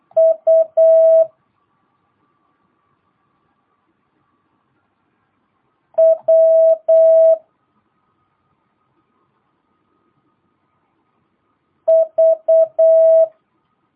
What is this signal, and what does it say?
Morse code, heard as a keyed tone, about 8 wpm, 650 Hz. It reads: U W V
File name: jh8cbh / 2022cw